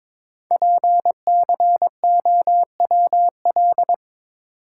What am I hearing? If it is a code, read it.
Morse code, heard as a keyed tone, about 22 wpm, 700 Hz. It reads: PCOWL
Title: Code Groups - 5 Letters - 22wpm